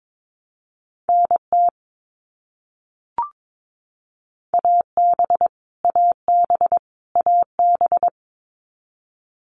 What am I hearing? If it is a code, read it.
Morse code, heard as a keyed tone, about 22 wpm, 700 Hz. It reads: NT AB AB AB
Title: Canadian Provinces and Territories - Repeated 3x - 22wpm